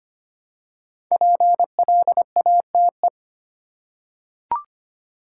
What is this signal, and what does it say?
Morse code, heard as a keyed tone, about 25 wpm, 700 Hz. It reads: PLATE
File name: 5 Letter Words - First 3 Letters Given - 25wpm